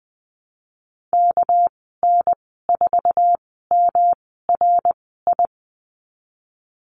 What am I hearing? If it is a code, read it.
Morse code, heard as a keyed tone, about 20 wpm, spaced as for 15 wpm, 700 Hz. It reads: KN4MRI
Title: 238 Lesson 38g - Letter q - Call Signs 20wpm